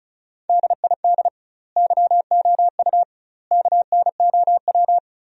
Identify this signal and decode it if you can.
Morse code, heard as a keyed tone, about 35 wpm, 700 Hz. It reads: DID YOU KNOW
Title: 2x Word-Spacing - Sentences from Top 100 Words - 35wpm